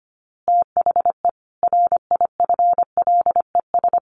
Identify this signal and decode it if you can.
Morse code, heard as a keyed tone, about 25 wpm, 700 Hz. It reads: THE RIFLES
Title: A Sound Of Thunder 25wpm